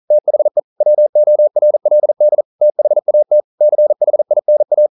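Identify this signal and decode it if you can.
Morse code, heard as a keyed tone, about 42 wpm, 600 Hz. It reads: THE WORLD THAT CHINA
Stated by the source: Morse code practice at 41 wpm, from CW med Gnister.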